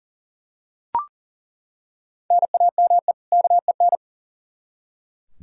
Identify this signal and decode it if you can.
Morse code, heard as a keyed tone, about 40 wpm, 700 Hz. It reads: NAME KEN
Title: Practice QSO - Exchange 1 - Beginning - 40wpm